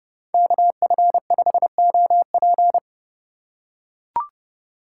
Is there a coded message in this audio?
Yes: Morse code from 0.3 to 2.8 s, heard as a keyed tone, about 29 wpm, 700 Hz. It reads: KF5OP